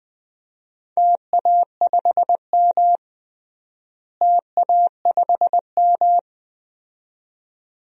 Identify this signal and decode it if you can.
Morse code, heard as a keyed tone, about 20 wpm, 700 Hz. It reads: TA5M TA5M
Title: DXCC-100 Call Signs - 4 Characters - Repeated 2x - 20wpm